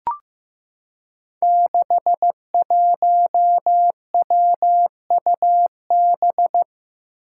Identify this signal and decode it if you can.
Morse code, heard as a keyed tone, about 15 wpm, 700 Hz. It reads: 61WUB
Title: Five Challenging Letter-Number Combinations 15wpm